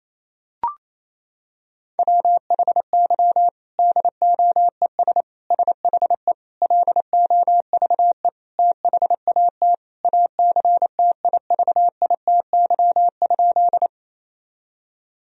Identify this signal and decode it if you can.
Morse code, heard as a keyed tone, about 28 wpm, 700 Hz. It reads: WHY DOES SHE LOVE THAT ACTIVITY?
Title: Sentences from Top 1000 Words - Review Part 4 - 28wpm